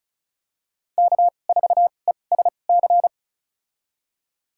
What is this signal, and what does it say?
Morse code, heard as a keyed tone, about 36 wpm, spaced as for 25 wpm, 700 Hz. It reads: K4ESC
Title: USA Call Signs - 5 Characters - 2x Character Spacing - 35wpm